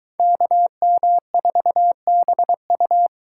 Morse code, heard as a keyed tone, about 23 wpm, 700 Hz. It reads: KM4BU